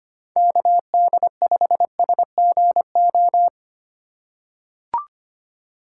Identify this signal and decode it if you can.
Morse code, heard as a keyed tone, about 25 wpm, 700 Hz. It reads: KD5SGO
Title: USA Call Signs - 6 Characters - 6x Character Spacing - 25wpm